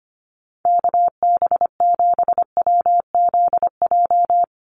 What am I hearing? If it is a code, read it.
Morse code, heard as a keyed tone, about 25 wpm, 700 Hz. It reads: KB7WZJ
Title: USA Call Signs 25wpm